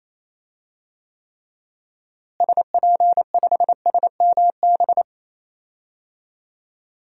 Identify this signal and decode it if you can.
Morse code, heard as a keyed tone, about 28 wpm, 700 Hz. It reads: SP5SMB
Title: DXCC-100 Call Signs - 6 Characters - Repeated 2x - 28wpm